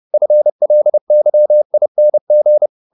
Morse code, heard as a keyed tone, about 30 wpm, 600 Hz. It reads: FLYING
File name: the old man and the sea / the old man and the sea 1 30wpm